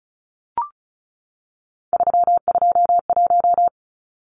Morse code, heard as a keyed tone, about 34 wpm, 700 Hz. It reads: 321
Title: Numbers - 3-Digit 35wpm